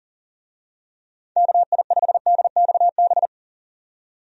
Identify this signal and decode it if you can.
Morse code, heard as a keyed tone, about 40 wpm, 700 Hz. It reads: KI5DXB
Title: USA Call Signs - 6 Characters - 40wpm